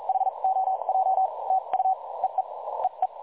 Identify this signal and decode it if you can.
Morse code, heard as a keyed tone, about 42 wpm, spaced as for 34 wpm, 760 Hz. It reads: HOJ TU EE EE
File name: dlazaawansowanych / nagraniazpasma / websdr recording 3565.0kHz